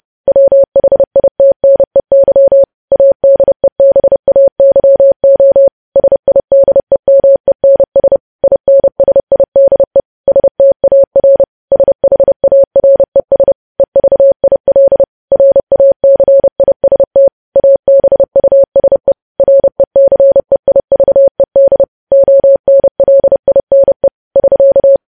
Morse code, heard as a keyed tone, about 30 wpm, 550 Hz. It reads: WHITNEY ADEBAYO SIDEMENS INSIDE STAR SHARES EVIL RACIST ABUSE RECEIVED ONLINE <SK>
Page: news headlines in morse code at 30 wpm